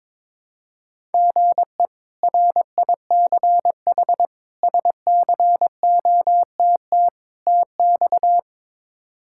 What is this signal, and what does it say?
Morse code, heard as a keyed tone, about 22 wpm, 700 Hz. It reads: GE RICH SCOTT TX
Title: K1USN SST QSO Practice - 22wpm